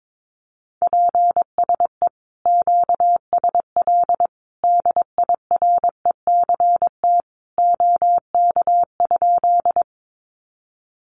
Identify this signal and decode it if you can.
Morse code, heard as a keyed tone, about 22 wpm, 700 Hz. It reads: PSE QSL DIRECT OK?
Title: Common QSO Phrases 22wpm